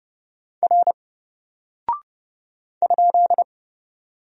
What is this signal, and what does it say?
Morse code, heard as a keyed tone, about 30 wpm, 700 Hz. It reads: R ?
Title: Single Character (Letter:Number:Punctuation:Prosign) 30wpm